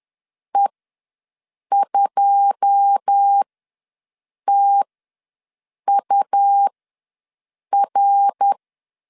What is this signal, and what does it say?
Morse code, heard as a keyed tone, about 10 wpm, spaced as for 6 wpm, 800 Hz. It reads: E2TUR
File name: LEZIONI / 9 2 ws 10wpm 4130